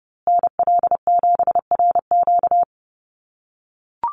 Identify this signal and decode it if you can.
Morse code, heard as a keyed tone, about 30 wpm, 700 Hz. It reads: NL7RQ
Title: USA Call Signs Repeated 3x 30wpm